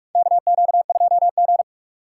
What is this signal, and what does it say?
Morse code, heard as a keyed tone, about 46 wpm, 700 Hz. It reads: KQ2G
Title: USA Call Signs - 2x Character Spacing - 45wpm